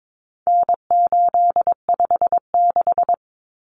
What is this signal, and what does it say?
Morse code, heard as a keyed tone, about 22 wpm, 700 Hz. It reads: N856
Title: Common Cut Numbers - 4-Digit 22wpm